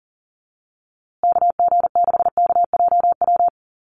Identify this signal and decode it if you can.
Morse code, heard as a keyed tone, about 40 wpm, 700 Hz. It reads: KG6KJW
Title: USA Call Signs Repeated 2x 40wpm